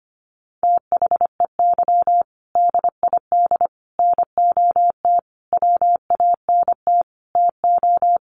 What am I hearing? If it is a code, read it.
Morse code, heard as a keyed tone, about 25 wpm, 700 Hz. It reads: THEY DID NOT WANT TO
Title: Sets of 5 Words 25wpm